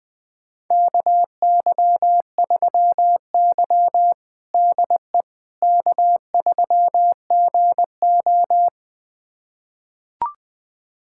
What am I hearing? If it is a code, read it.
Morse code, heard as a keyed tone, about 20 wpm, 700 Hz. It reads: KY3Y DE K3GO
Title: USA Callsign DE Callsign - 4 Characters - 20wpm